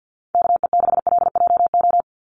Morse code, heard as a keyed tone, about 50 wpm, 700 Hz. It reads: KE6LJO
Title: USA Call Signs 50wpm